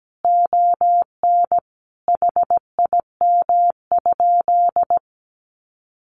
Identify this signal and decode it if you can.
Morse code, heard as a keyed tone, about 17 wpm, 700 Hz. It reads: ON HIM?
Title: Sentences from Top 1100 Words 17wpm